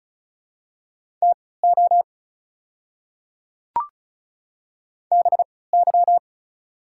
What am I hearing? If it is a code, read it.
Morse code, heard as a keyed tone, about 36 wpm, spaced as for 20 wpm, 700 Hz. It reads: TO BY
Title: 3x Character Spacing - Top 100 Words - 35wpm